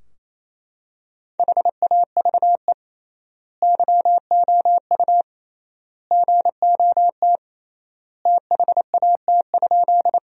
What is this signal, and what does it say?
Morse code, heard as a keyed tone, about 28 wpm, 700 Hz. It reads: HAVE YOU GOT THAT?
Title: 3x Word-Spacing - Sentences from Top 300 Words - 28wpm